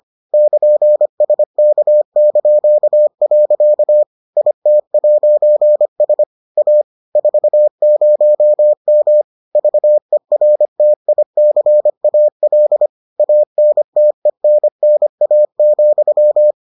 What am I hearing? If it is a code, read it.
Morse code, heard as a keyed tone, about 25 wpm, 600 Hz. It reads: <KN>SK). IT'S A 40M VERTICAL ANTENNA,